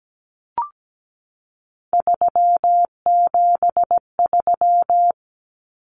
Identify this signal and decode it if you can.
Morse code, heard as a keyed tone, about 17 wpm, 700 Hz. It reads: 373
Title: Numbers - 3-Digit 17wpm